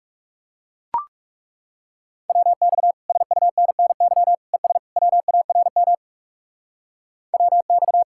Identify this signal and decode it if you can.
Morse code, heard as a keyed tone, about 37 wpm, 700 Hz. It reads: WX SUNNY ES WARM WX
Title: Speed-Racing Common QSO Phrases - 30wpm